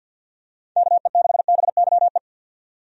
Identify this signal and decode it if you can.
Morse code, heard as a keyed tone, about 50 wpm, 700 Hz. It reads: KE6BYE
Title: USA Call Signs - 6 Characters - 50wpm